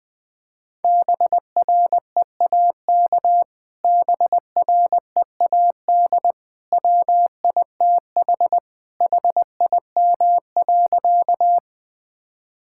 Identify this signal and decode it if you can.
Morse code, heard as a keyed tone, about 20 wpm, 700 Hz. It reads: BREAK BREAD WITH HIM.
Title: Sentences from Top 1600 Words 20wpm